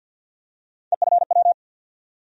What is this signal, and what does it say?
Morse code, heard as a keyed tone, about 52 wpm, 700 Hz. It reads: EFW
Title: USA Call Sign Postfixes - 3 Letters 50wpm